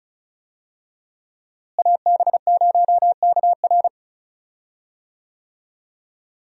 Morse code, heard as a keyed tone, about 36 wpm, 700 Hz. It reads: AB0KR
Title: USA Call Signs - 5 Characters - Repeated 2x - 35wpm